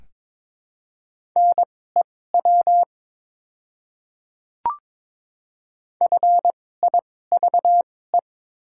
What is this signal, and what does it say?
Morse code, heard as a keyed tone, about 21 wpm, spaced as for 16 wpm, 700 Hz. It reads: NEW FIVE
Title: 2x Character Spacing - Top 500 Words - 22wpm